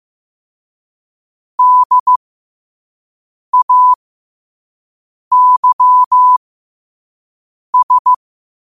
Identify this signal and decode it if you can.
Morse code, heard as a keyed tone, about 15 wpm, spaced as for 5 wpm, 1000 Hz. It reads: DAYS
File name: 5wpm